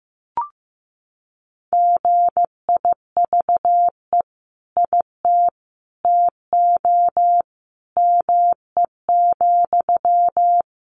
Morse code, heard as a keyed tone, about 15 wpm, 700 Hz. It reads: GIVE IT TO ME,
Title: Sentences from Top 200 Words 15wpm